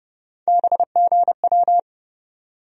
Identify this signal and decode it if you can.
Morse code, heard as a keyed tone, about 32 wpm, 700 Hz. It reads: BGW
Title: USA Call Sign Postfixes - 3 Letters 30wpm